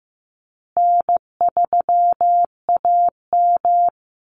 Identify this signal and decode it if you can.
Morse code, heard as a keyed tone, about 15 wpm, 700 Hz. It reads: N3AM
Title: Easy Call Signs Repeated 3x - 15wpm